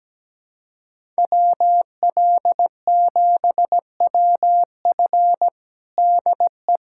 Morse code, heard as a keyed tone, about 17 wpm, 700 Hz. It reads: WL7WF DE
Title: USA Callsign DE Callsign 17wpm